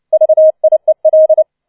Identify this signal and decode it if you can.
Morse code, heard as a keyed tone, about 29 wpm, 630 Hz. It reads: VIEL
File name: 006 Sechster Streich 30wpm